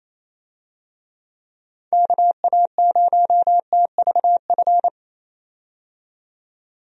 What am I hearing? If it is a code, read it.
Morse code, heard as a keyed tone, about 28 wpm, 700 Hz. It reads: KA0TVF
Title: USA Call Signs - 6 Characters - 28wpm